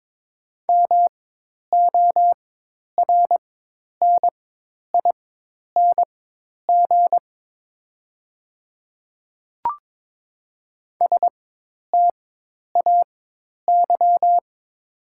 Morse code, heard as a keyed tone, about 22 wpm, spaced as for 10 wpm, 700 Hz. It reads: MORNING STAY
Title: 4x Character Spacing - Top 500 Words - 22wpm